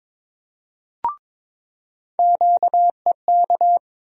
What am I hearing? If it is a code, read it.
Morse code, heard as a keyed tone, about 21 wpm, 700 Hz. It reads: QEK
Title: USA Call Sign Postfixes - 3 Letters 22wpm